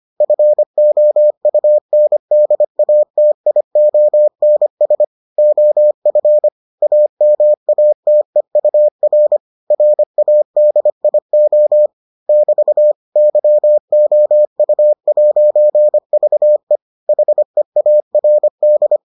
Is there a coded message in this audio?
Yes: Morse code from 0.2 to 19.0 s, heard as a keyed tone, about 25 wpm, 600 Hz. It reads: FOUNDATIONS OF AMATEUR RADIO <BT> YOU'VE HEARD